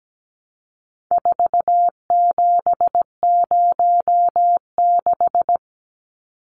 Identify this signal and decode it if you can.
Morse code, heard as a keyed tone, about 17 wpm, 700 Hz. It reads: 4706
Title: Numbers - 4-Digit 17wpm